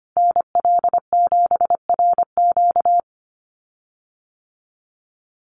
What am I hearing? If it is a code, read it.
Morse code, heard as a keyed tone, about 25 wpm, 700 Hz. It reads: NL7RQ
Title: USA Call Signs Repeated 3x 25wpm